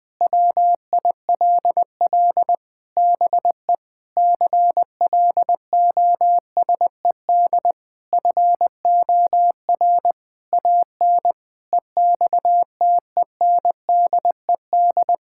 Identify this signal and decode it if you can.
Morse code, heard as a keyed tone, about 21 wpm, 700 Hz. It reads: WILL BE CLOSED FOR AN EXTENDED
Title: Sentences with 6-Word Phrases - 20wpm